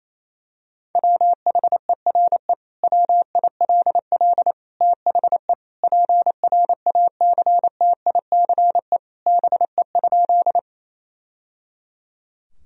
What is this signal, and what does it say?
Morse code, heard as a keyed tone, about 28 wpm, 700 Hz. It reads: WHERE WILL THE PRACTICE BE?
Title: Sentences from Top 1000 Words - Review Part 4 - 28wpm